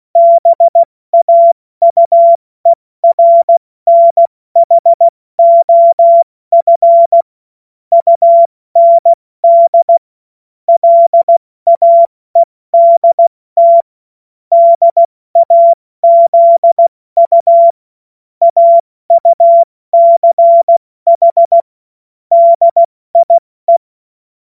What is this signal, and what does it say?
Morse code, heard as a keyed tone, about 16 wpm, spaced as for 14 wpm, 680 Hz. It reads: BAUERNHOF UND LAEDT DAZU AUCH DIE